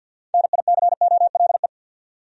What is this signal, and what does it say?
Morse code, heard as a keyed tone, about 51 wpm, 700 Hz. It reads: NICOLE
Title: USA Common Names - 50wpm